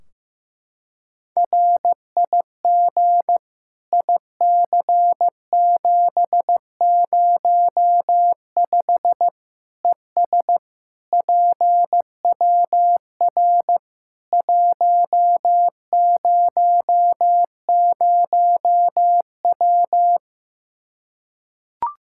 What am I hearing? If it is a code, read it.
Morse code, heard as a keyed tone, about 15 wpm, 700 Hz. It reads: RIG IC705 ES PWR 100W
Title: Practice QSO - Exchange 2 - Info - 15wpm